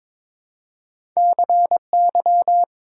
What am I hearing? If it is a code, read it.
Morse code, heard as a keyed tone, about 23 wpm, 700 Hz. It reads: CY
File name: practice / 22wpm / Two Letter-Number Combinations - 22wpm